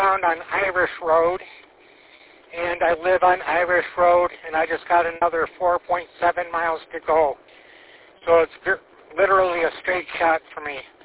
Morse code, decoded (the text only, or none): none